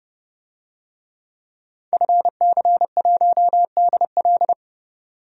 Morse code, heard as a keyed tone, about 32 wpm, 700 Hz. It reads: FC1DL
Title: DXCC-100 Call Signs - 5 Characters - 30wpm